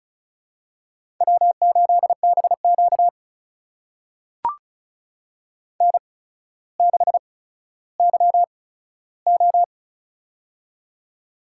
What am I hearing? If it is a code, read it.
Morse code, heard as a keyed tone, about 36 wpm, spaced as for 15 wpm, 700 Hz. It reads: W8BQ N6YO
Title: USA Call Signs - 4 Characters - 8x Character Spacing - 35wpm